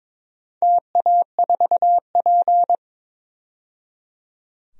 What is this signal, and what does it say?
Morse code, heard as a keyed tone, about 22 wpm, 700 Hz. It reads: TA4P
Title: DXCC-100 Call Signs - 4 Characters - 22wpm